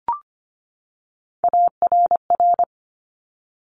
Morse code, heard as a keyed tone, about 25 wpm, 700 Hz. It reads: ARR
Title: Top Three Letter Combinations in Words 25wpm